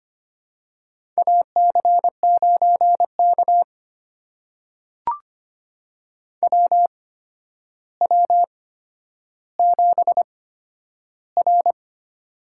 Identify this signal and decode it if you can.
Morse code, heard as a keyed tone, about 25 wpm, spaced as for 11 wpm, 700 Hz. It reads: AC9K WW7R
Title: USA Call Signs - 8x Character Spacing - 25wpm